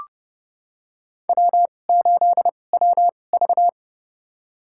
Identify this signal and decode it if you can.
Morse code, heard as a keyed tone, about 30 wpm, spaced as for 22 wpm, 700 Hz. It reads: W8WV
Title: USA Call Signs - 4 Characters - 2x Character Spacing - 30wpm